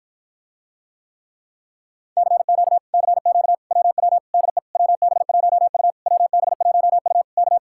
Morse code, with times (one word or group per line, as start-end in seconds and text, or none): 2.2-2.8 CQ
2.9-3.5 CQ
3.7-4.2 WW
4.3-4.6 DE
4.8-5.9 WB1U
6.1-7.2 WB1U
7.4-7.6 K